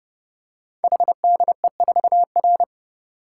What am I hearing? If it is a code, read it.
Morse code, heard as a keyed tone, about 28 wpm, 700 Hz. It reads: HDE4R
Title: Code Groups - 5 Letter:Numbers - 30wpm